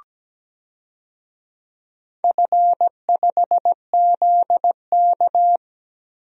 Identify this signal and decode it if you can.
Morse code, heard as a keyed tone, about 17 wpm, 700 Hz. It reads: F5ZK
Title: DXCC-100 Call Signs - 4 Characters - Repeated 2x - 17wpm